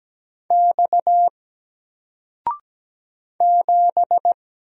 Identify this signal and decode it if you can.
Morse code, heard as a keyed tone, about 17 wpm, 700 Hz. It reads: X 7
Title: Single Letter-Number 17wpm